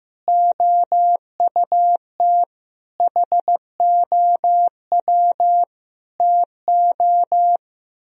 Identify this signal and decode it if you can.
Morse code, heard as a keyed tone, about 15 wpm, 700 Hz. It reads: OUT HOW TO
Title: Speed-Racing - Sets of 5 Words 15wpm